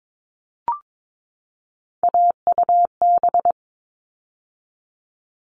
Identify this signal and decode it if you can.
Morse code, heard as a keyed tone, about 22 wpm, 700 Hz. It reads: AUB
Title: All Cut Numbers - 3-Digit 22wpm